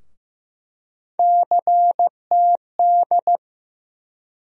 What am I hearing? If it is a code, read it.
Morse code, heard as a keyed tone, about 15 wpm, 700 Hz. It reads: CTD
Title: USA Call Sign Postfixes - 3 Letters 15wpm